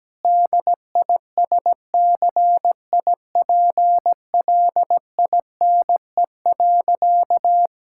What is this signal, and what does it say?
Morse code, heard as a keyed tone, about 17 wpm, 700 Hz. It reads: DISCIPLINE.
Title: Sentences from Top 1000 Words - Review Part 6 - 17wpm